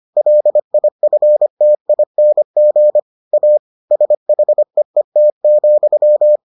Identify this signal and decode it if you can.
Morse code, heard as a keyed tone, about 25 wpm, 600 Hz. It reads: LIFTING A SHEET,